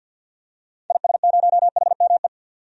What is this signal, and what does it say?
Morse code, heard as a keyed tone, about 46 wpm, 700 Hz. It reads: IS0HME
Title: DXCC-100 Call Signs - 6 Characters - 50wpm